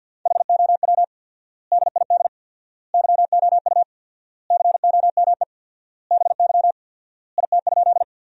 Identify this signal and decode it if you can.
Morse code, heard as a keyed tone, about 50 wpm, 700 Hz. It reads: HOW DID YOU COME BY IT?
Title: 4x Word-Spacing - Sentences from Top 100 Words - 50wpm